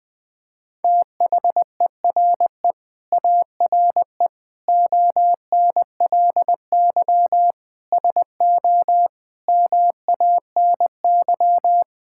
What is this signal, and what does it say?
Morse code, heard as a keyed tone, about 20 wpm, 700 Hz. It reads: THERE ARE ONLY SO MANY
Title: Sets of 9 Words - 20wpm